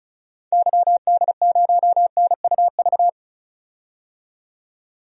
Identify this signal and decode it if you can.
Morse code, heard as a keyed tone, about 34 wpm, 700 Hz. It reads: YD0NUV
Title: DXCC-100 Call Signs - 6 Characters - 35wpm